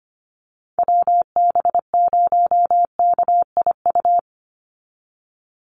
Morse code, heard as a keyed tone, about 25 wpm, 700 Hz. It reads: WB0KIU
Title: NA SOTA Chaser Callsigns 25wpm